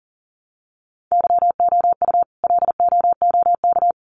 Morse code, heard as a keyed tone, about 40 wpm, 700 Hz. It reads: YOU LOOK
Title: Sets of 4 Words 40wpm